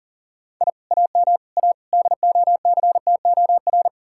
Morse code, heard as a keyed tone, about 51 wpm, 700 Hz. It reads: I AM A DOCTOR